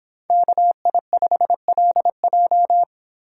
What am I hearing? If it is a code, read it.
Morse code, heard as a keyed tone, about 26 wpm, 700 Hz. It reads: KI5LJ